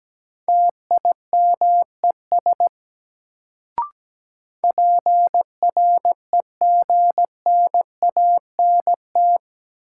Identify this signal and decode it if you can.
Morse code, heard as a keyed tone, about 17 wpm, 700 Hz. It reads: TIMES PREGNANT
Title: Sentences from Top 1000 Words - Review Part 6 - New Word Drill - 17wpm